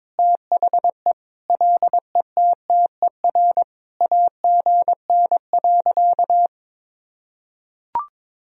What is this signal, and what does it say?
Morse code, heard as a keyed tone, about 22 wpm, 700 Hz. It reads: THE LETTER AGN.
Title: Sentences from Top 300 Words and Abbreviations 22wpm